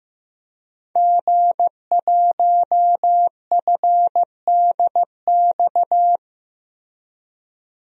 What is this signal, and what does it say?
Morse code, heard as a keyed tone, about 15 wpm, 700 Hz. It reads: G1FDX
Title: DXCC-100 Call Signs - 5 Characters - Repeated 3x - 15wpm